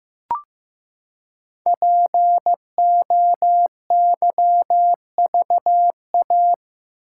Morse code, heard as a keyed tone, about 15 wpm, 700 Hz. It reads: POYVA
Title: Code Groups - 5 Letters - 15wpm